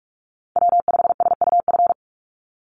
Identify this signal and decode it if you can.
Morse code, heard as a keyed tone, about 44 wpm, 700 Hz. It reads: W5SUF
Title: USA Call Signs 45wpm